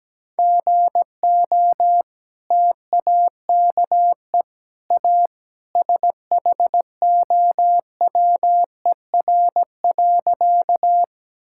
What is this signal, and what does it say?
Morse code, heard as a keyed tone, about 17 wpm, 700 Hz. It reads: GO TAKE A SHOWER.
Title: Sentences from Top 1600 Words 17wpm